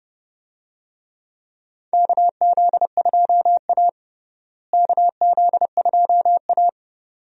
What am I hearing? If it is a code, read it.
Morse code, heard as a keyed tone, about 30 wpm, 700 Hz. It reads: KZ2A KZ2A
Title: USA Call Signs - 4 Characters - Repeated 2x - 30wpm